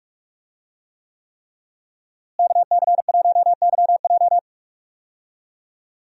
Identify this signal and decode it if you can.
Morse code, heard as a keyed tone, about 47 wpm, 700 Hz. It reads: KC1YJ
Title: USA Call Signs - 5 Characters - Repeated 3x - 45wpm